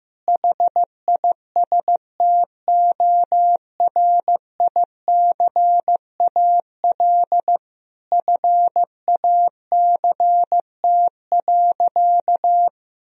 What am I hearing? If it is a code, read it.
Morse code, heard as a keyed tone, about 15 wpm, 700 Hz. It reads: HISTORICAL FACT.
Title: Sentences from Top 1000 Words - Review Part 4 - 15wpm